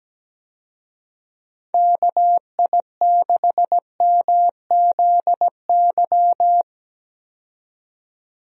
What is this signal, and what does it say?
Morse code, heard as a keyed tone, about 17 wpm, 700 Hz. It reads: KI6MZY
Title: USA Call Signs - 6 Characters - 17wpm